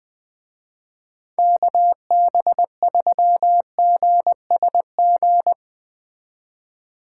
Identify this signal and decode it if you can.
Morse code, heard as a keyed tone, about 20 wpm, 700 Hz. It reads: KB3GSG
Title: Speed Racing - USA 2x3 Call Signs 17wpm